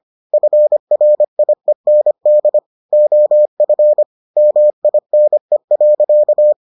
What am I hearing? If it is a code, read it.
Morse code, heard as a keyed tone, about 25 wpm, 600 Hz. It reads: FRIEND OF MINE.